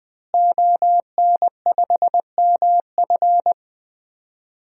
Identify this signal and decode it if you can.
Morse code, heard as a keyed tone, about 20 wpm, 700 Hz. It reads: ON5MF
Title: DXCC-100 Call Signs - 5 Characters - Repeated 3x - 20wpm